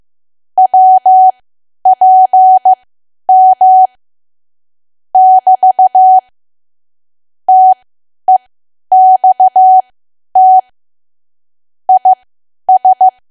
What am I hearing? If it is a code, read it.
Morse code, heard as a keyed tone, about 15 wpm, spaced as for 10 wpm, 750 Hz. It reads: WPM <BT> TEXT IS
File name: Archive / 10 WPM / 10WPM